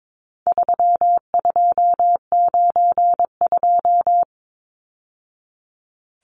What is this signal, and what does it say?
Morse code, heard as a keyed tone, about 22 wpm, 700 Hz. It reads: 3292
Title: Numbers - 4-Digit 22wpm